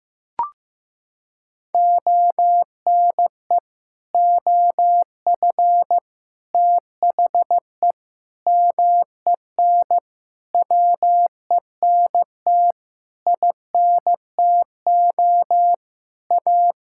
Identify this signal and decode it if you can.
Morse code, heard as a keyed tone, about 15 wpm, 700 Hz. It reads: ONE OF THE MEN WENT INTO A
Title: Sentences from Top 1600 Words 15wpm